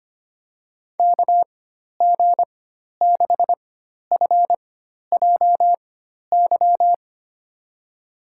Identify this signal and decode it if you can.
Morse code, heard as a keyed tone, about 25 wpm, spaced as for 12 wpm, 700 Hz. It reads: KG6FJY
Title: USA Call Signs - 6 Characters - 4x Character Spacing - 25wpm